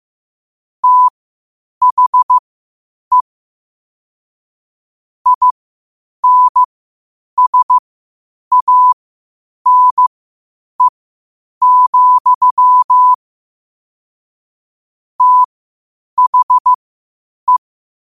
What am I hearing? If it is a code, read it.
Morse code, heard as a keyed tone, about 15 wpm, spaced as for 8 wpm, 1000 Hz. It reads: THE INSANE, THE
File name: qotdfiles / 8wpm